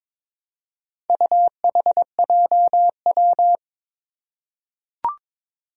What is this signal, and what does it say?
Morse code, heard as a keyed tone, about 22 wpm, 700 Hz. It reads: UHJW
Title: Four Challenging Letter-Number Combinations 22wpm